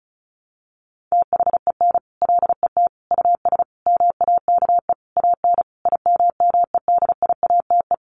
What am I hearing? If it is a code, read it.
Morse code, heard as a keyed tone, about 34 wpm, 700 Hz. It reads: THEN LET US MAKE AN IMMEDIATE
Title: Sentences from Top 1200 Words 35wpm